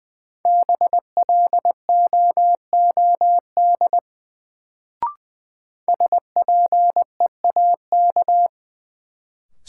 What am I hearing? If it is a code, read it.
Morse code, heard as a keyed tone, about 20 wpm, 700 Hz. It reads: BLOOD SPEAK
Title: Sentences from Top 1100 Words - New Word Drill Alt - 20wpm